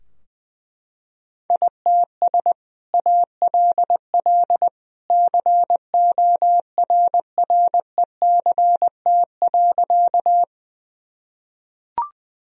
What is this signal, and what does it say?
Morse code, heard as a keyed tone, about 21 wpm, 700 Hz. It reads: ITS ALL CORRECT.
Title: Sentences from Top 1000 Words - Review Part 2 - 20wpm